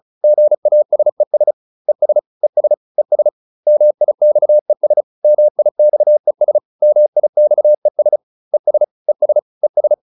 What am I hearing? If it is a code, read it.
Morse code, heard as a keyed tone, about 34 wpm, 600 Hz. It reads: GASES ES ES ES MIXES MIXES MIXES ES ES ES